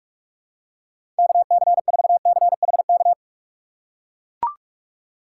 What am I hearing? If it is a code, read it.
Morse code, heard as a keyed tone, about 42 wpm, 700 Hz. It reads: KC4CHK
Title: USA Call Signs - 3x Character Spacing - 45wpm